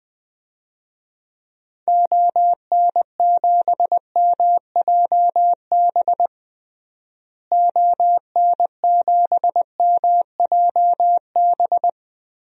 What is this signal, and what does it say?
Morse code, heard as a keyed tone, about 21 wpm, 700 Hz. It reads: ON7MJB ON7MJB
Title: DXCC-100 Call Signs - 6 Characters - Repeated 3x - 20wpm